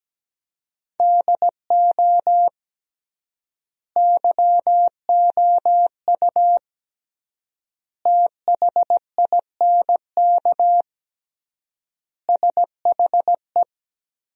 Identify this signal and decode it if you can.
Morse code, heard as a keyed tone, about 17 wpm, 700 Hz. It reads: DO YOU THINK SHE
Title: 3x Word-Spacing - Sentences from Top 200 Words - 17wpm